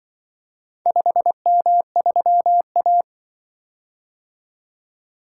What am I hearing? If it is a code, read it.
Morse code, heard as a keyed tone, about 24 wpm, 700 Hz. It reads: 5M3A
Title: Speed-Racing - Four Letter-Number Combinations - 20wpm